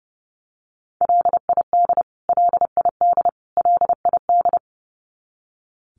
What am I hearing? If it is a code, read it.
Morse code, heard as a keyed tone, about 30 wpm, 700 Hz. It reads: LID LID LID
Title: Abbreviations Repeated 3x - 30wpm